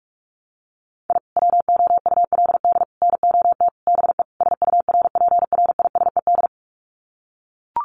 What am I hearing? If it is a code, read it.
Morse code, heard as a keyed tone, about 44 wpm, 700 Hz. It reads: I WOULD NOT BE SURPRISED
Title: Sets of 5 Words Encore2 45wpm